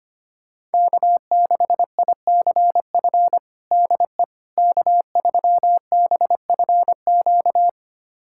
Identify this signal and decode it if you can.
Morse code, heard as a keyed tone, about 25 wpm, 700 Hz. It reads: K6ICF DE K3BFQ
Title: USA Callsign DE Callsign - 5 Characters - 25wpm